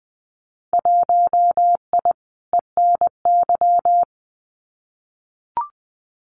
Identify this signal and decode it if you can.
Morse code, heard as a keyed tone, about 20 wpm, 700 Hz. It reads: 1I ENY
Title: Winter Field Day Exchanges Encore 20wpm